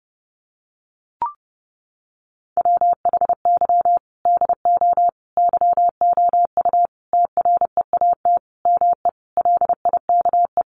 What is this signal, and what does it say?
Morse code, heard as a keyed tone, about 28 wpm, 700 Hz. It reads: WHY DO YOU TREAT ME LIKE
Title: Sentences from Top 1300 Words 30wpm